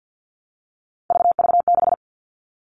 Morse code, heard as a keyed tone, about 50 wpm, 700 Hz. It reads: VV6
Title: All Cut Numbers - 3-Digit 50wpm